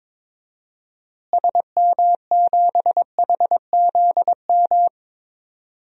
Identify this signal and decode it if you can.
Morse code, heard as a keyed tone, about 22 wpm, 700 Hz. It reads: SM7HZM
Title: DXCC-100 Call Signs - 6 Characters - Repeated 3x - 22wpm